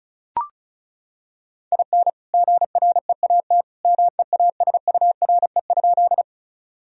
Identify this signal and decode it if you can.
Morse code, heard as a keyed tone, about 36 wpm, 700 Hz. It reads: IN GREAT MEASURE?
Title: Sentences from Top 1000 Words - Review Part 2 - 35wpm